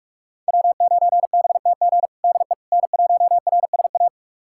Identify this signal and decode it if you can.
Morse code, heard as a keyed tone, about 46 wpm, 700 Hz. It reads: W9BTG DE N1RSA
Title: USA Callsign DE Callsign - 5 Characters - 45wpm